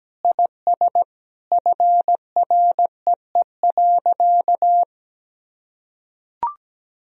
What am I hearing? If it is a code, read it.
Morse code, heard as a keyed tone, about 17 wpm, 700 Hz. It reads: IS FREE.
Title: Sentences from Top 1000 Words - Review Part 1 - 17wpm